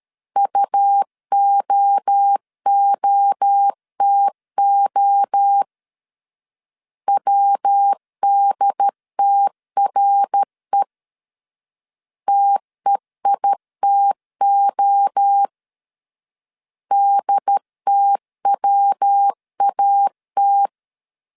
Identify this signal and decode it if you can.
Morse code, heard as a keyed tone, about 13 wpm, 800 Hz. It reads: UOOTO WDTRE TEITO DTWAT
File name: CW / DET / 1 3 1500C S B 12wpm 2828